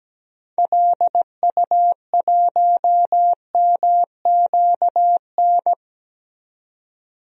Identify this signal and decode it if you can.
Morse code, heard as a keyed tone, about 17 wpm, 700 Hz. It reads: LU1MQN
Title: DXCC-100 Call Signs - 6 Characters - 17wpm